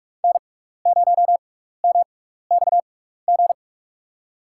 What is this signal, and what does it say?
Morse code, heard as a keyed tone, about 42 wpm, spaced as for 15 wpm, 700 Hz. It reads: N0MXG
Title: USA Call Signs - 5 Characters - 6x Character Spacing - 45wpm